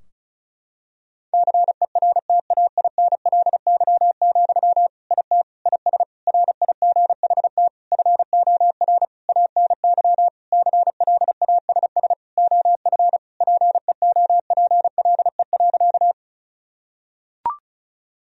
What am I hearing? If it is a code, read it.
Morse code, heard as a keyed tone, about 34 wpm, 700 Hz. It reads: CERTAINLY, IT IS RIGHT FOR ANY CLASS OF PEOPLE.
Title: Sentences from Top 1000 Words - Review Part 2 - 35wpm